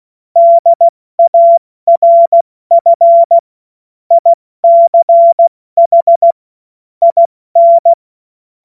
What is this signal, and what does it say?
Morse code, heard as a keyed tone, about 16 wpm, 680 Hz. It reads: DARF ICH IN
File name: Morsekurs / Witze / Witz 07. September 2021, 70 F80